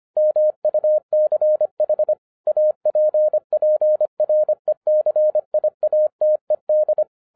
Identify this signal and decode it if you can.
Morse code, heard as a keyed tone, about 25 wpm, 610 Hz. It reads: MUCH APPRECIATED
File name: 2011-07-mp3 / n59 BACK FROM A BIG ACCIDENT 25wpm